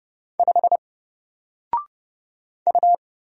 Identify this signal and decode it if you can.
Morse code, heard as a keyed tone, about 30 wpm, 700 Hz. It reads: 5 U
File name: Single-Letter-Number-30wpm